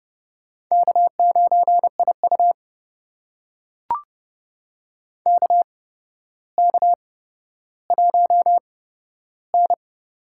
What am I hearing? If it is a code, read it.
Morse code, heard as a keyed tone, about 30 wpm, spaced as for 13 wpm, 700 Hz. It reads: K9IU KK1N
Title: USA Call Signs - 4 Characters - 8x Character Spacing - 30wpm